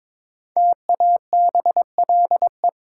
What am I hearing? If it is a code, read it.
Morse code, heard as a keyed tone, about 22 wpm, 700 Hz. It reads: TABLE
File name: 5 Letter Words - First 3 Letters Given - 22wpm